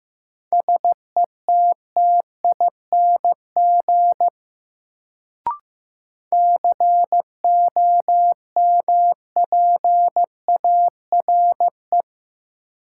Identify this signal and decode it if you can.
Morse code, heard as a keyed tone, about 15 wpm, 700 Hz. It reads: SETTING COMPARE
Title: Sentences from Top 1300 Words - New Word Drill - 15wpm